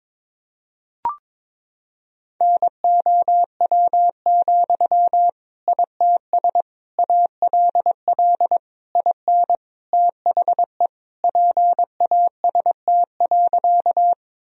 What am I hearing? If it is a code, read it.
Morse code, heard as a keyed tone, about 22 wpm, 700 Hz. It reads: NOW, ITS ALL IN THE PAST.
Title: Sentences from Top 1000 Words - Review Part 3 - 22wpm